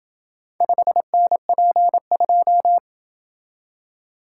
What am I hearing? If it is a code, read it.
Morse code, heard as a keyed tone, about 28 wpm, 700 Hz. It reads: 5NP2